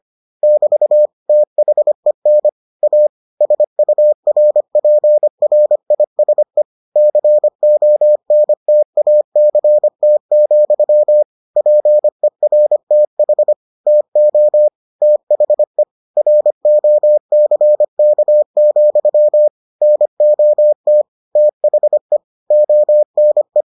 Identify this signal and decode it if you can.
Morse code, heard as a keyed tone, about 25 wpm, 600 Hz. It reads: <BT> THEN A SURPRISE CONTACT, PERTH TO THE ROCK, NOT THE ONE